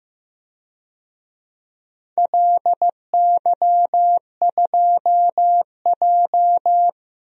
Morse code, heard as a keyed tone, about 15 wpm, 700 Hz. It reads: LY2J
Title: DXCC-100 Call Signs - 4 Characters - 15wpm